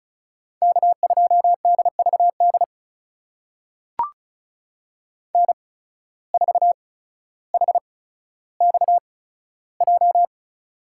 Morse code, heard as a keyed tone, about 34 wpm, spaced as for 15 wpm, 700 Hz. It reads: K2DVD N4HXJ
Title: USA Call Signs - 5 Characters - 8x Character Spacing - 35wpm